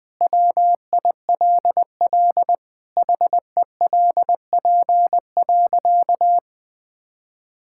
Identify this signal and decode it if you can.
Morse code, heard as a keyed tone, about 20 wpm, 700 Hz. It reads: WILL HELP.
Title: Sentences with 3-Word Phrases - Spoken Beginning - 20wpm